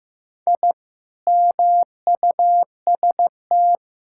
Morse code, heard as a keyed tone, about 15 wpm, 700 Hz. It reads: I MUST
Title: Sentences from Top 1000 Words - Review Part 6 - 15wpm